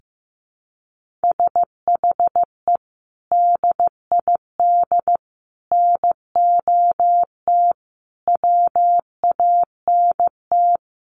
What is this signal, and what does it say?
Morse code, heard as a keyed tone, about 15 wpm, 700 Hz. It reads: SHE DID NOT WANT
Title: Sets of 4 Words 15wpm